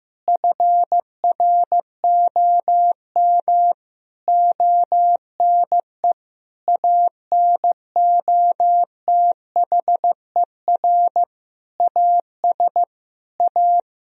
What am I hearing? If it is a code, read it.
Morse code, heard as a keyed tone, about 15 wpm, 700 Hz. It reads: FROM ONE ANOTHER AS A
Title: Sentences from Top 1000 Words - Review Part 1 - 15wpm